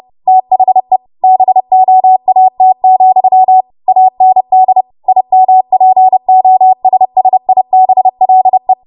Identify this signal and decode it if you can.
Morse code, heard as a keyed tone, about 30 wpm, 750 Hz. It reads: THE BOAT, AND IMPOSSIBLE
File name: Morse / 30WPM